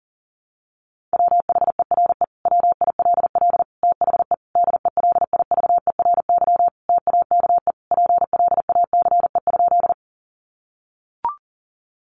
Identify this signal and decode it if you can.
Morse code, heard as a keyed tone, about 40 wpm, 700 Hz. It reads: WHERE WILL THE DELIVERY TAKE PLACE?
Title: Sentences from Top 1500 Words 40wpm